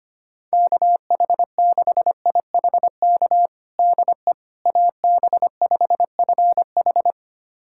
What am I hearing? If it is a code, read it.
Morse code, heard as a keyed tone, about 25 wpm, 700 Hz. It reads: KH6IHK DE AB5FH
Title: USA Callsign DE Callsign 25wpm